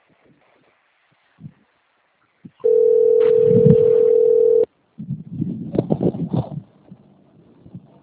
Ringback tone (North America), 440 + 480 Hz.
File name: out- - - - - .64921